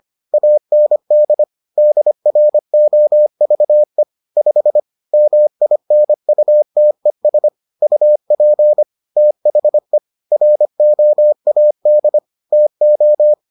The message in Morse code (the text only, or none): AND DROVE 5 MINUTES UP THE ROAD TO